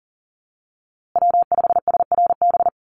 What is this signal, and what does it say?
Morse code, heard as a keyed tone, about 40 wpm, 700 Hz. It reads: W5SRB
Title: USA Call Signs Repeated 3x 40wpm